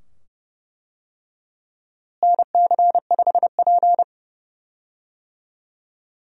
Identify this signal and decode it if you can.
Morse code, heard as a keyed tone, about 31 wpm, 700 Hz. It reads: NC5P
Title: USA Call Signs - 4 Characters - Repeated 2x - 30wpm